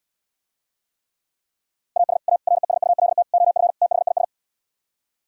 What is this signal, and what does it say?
Morse code, heard as a keyed tone, about 77 wpm, 700 Hz. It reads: AS A RESULT OF THIS